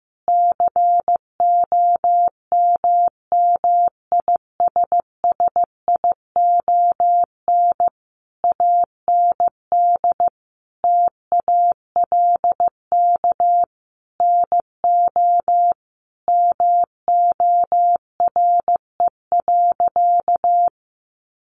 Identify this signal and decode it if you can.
Morse code, heard as a keyed tone, about 15 wpm, 700 Hz. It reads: COMMISSION AND TALK NO MORE.
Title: Sentences from Top 1400 Words 15wpm